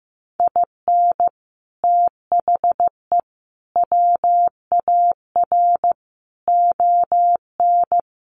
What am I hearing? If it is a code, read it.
Morse code, heard as a keyed tone, about 15 wpm, 700 Hz. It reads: IN THE WAR ON
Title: Sets of 5 Words Encore2 15wpm